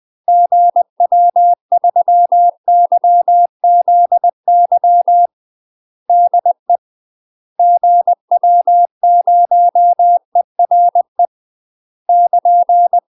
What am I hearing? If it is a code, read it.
Morse code, heard as a keyed tone, about 20 wpm, 700 Hz. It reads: GW3YZY DE GW0ERE <KN>
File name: UK MASTER QSOs TRAINING / 20WPM / UK MASTER QSOs 0068